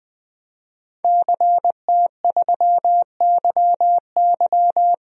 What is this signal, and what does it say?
Morse code, heard as a keyed tone, about 20 wpm, 700 Hz. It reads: CT3YY
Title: DXCC-100 Call Signs - 5 Characters - Repeated 3x - 20wpm